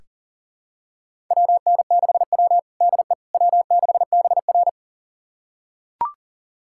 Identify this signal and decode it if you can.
Morse code, heard as a keyed tone, about 40 wpm, 700 Hz. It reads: WN6W DE W6BR
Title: USA Callsign DE Callsign - 4 Characters - 40wpm